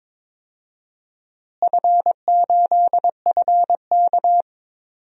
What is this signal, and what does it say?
Morse code, heard as a keyed tone, about 22 wpm, 700 Hz. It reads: F8FK
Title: DXCC-100 Call Signs - 4 Characters - Repeated 2x - 22wpm